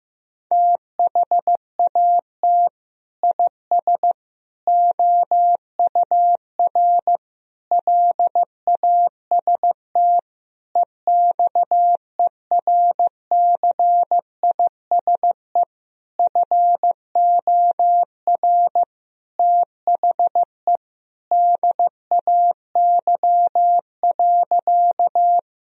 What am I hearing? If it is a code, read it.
Morse code, heard as a keyed tone, about 16 wpm, 700 Hz. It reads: THAT IS OUR LAST EXERCISE FOR THE DAY.
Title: Sentences from Top 1000 Words - Review Part 4 - 15wpm